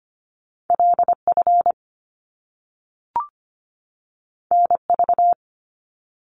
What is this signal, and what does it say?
Morse code, heard as a keyed tone, about 25 wpm, 700 Hz. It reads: LF NV
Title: Top Two Letter Combinations in Words 25wpm